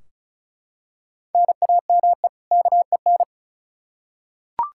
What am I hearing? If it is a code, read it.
Morse code, heard as a keyed tone, about 34 wpm, 700 Hz. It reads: NAME KEN
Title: Practice QSO - Exchange 1 - Beginning - 35wpm